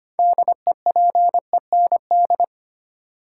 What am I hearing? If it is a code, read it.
Morse code, heard as a keyed tone, about 25 wpm, 700 Hz. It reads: DEPEND
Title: Sentences from Top 1300 Words - New Word Drill - 25wpm